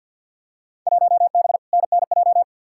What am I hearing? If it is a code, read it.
Morse code, heard as a keyed tone, about 47 wpm, 700 Hz. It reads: 1B NNJ
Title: Field Day Exchanges Encore - 50wpm